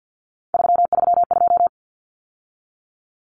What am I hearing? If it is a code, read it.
Morse code, heard as a keyed tone, about 50 wpm, 700 Hz. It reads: 332
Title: Numbers - 3-Digit 50wpm